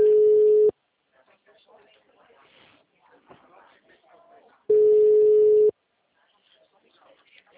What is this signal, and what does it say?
Ringback tone (Europe, 425 Hz), with speech.